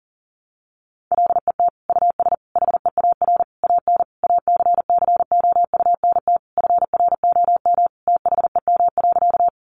Morse code, heard as a keyed tone, about 40 wpm, 700 Hz. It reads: LET US HEAR AN ACCOUNT FROM THEM.
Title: Sentences from Top 900 Words 40wpm